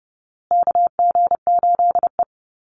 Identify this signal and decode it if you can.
Morse code, heard as a keyed tone, about 30 wpm, 700 Hz. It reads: KG8E
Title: USA Call Signs Repeated 3x 30wpm